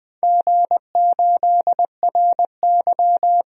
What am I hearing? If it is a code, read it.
Morse code, heard as a keyed tone, about 20 wpm, 700 Hz. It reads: G8RY
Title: DXCC-100 Call Signs - 4 Characters - 20wpm